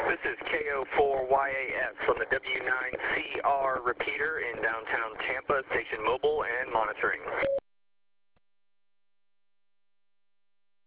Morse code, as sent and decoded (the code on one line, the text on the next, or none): none